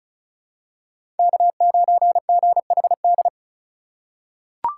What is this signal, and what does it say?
Morse code, heard as a keyed tone, about 34 wpm, 690 Hz. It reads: K9GHD
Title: USA Call Signs - 5 Characters - 3x Character Spacing - 35wpm